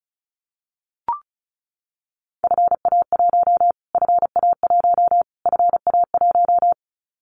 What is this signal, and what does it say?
Morse code, heard as a keyed tone, about 34 wpm, 700 Hz. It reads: FA1 FA1 FA1
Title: Callsign Prefixes for Top 25 DXCC - Repeated 3x - 35wpm